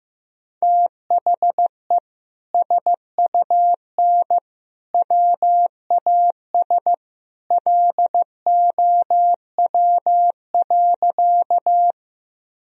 Morse code, heard as a keyed tone, about 15 wpm, 700 Hz. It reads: THE SUN WAS LOW.
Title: Sentences from Top 1000 Words - Review Part 1 - 15wpm